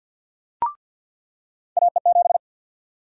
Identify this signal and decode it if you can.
Morse code, heard as a keyed tone, about 53 wpm, 700 Hz. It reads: AE7
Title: USA Call Sign Prefixes - 50wpm